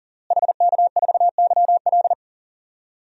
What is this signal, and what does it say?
Morse code, heard as a keyed tone, about 40 wpm, 700 Hz. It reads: HK4YL
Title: DXCC-100 Call Signs - 5 Characters - Repeated 3x - 40wpm